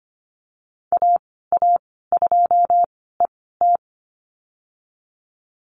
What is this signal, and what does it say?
Morse code, heard as a keyed tone, about 25 wpm, spaced as for 16 wpm, 700 Hz. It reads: AA2ET
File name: Lesson 15g - Number 5 - Call Signs 25wpm